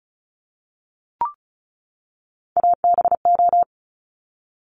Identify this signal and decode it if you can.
Morse code, heard as a keyed tone, about 36 wpm, 700 Hz. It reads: ABO
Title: Top Three Letter Combinations in Words 35wpm